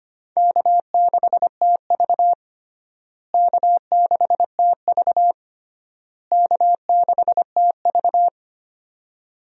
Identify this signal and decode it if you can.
Morse code, heard as a keyed tone, about 25 wpm, 700 Hz. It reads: K6TV K6TV K6TV
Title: USA Call Signs - 4 Characters - Repeated 3x - 25wpm